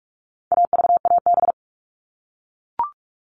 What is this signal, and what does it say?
Morse code, heard as a keyed tone, about 44 wpm, 700 Hz. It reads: AVAB
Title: All Cut Numbers - 4-Digit 45wpm